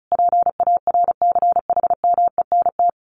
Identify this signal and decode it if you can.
Morse code, heard as a keyed tone, about 34 wpm, 700 Hz. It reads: PARCHMENT
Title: A Sound Of Thunder 35wpm